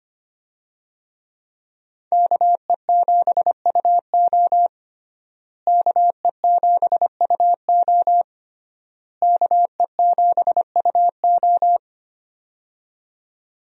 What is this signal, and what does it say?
Morse code, heard as a keyed tone, about 25 wpm, 700 Hz. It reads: KE7UO KE7UO KE7UO
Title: USA Call Signs - 5 Characters - Repeated 3x - 25wpm